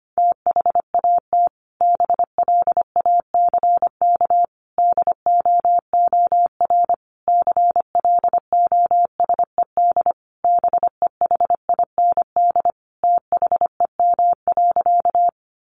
Morse code, heard as a keyed tone, about 25 wpm, 700 Hz. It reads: THAT BLACK DOOR CLOSED BEHIND THEM.
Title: Sentences from Top 1400 Words 25wpm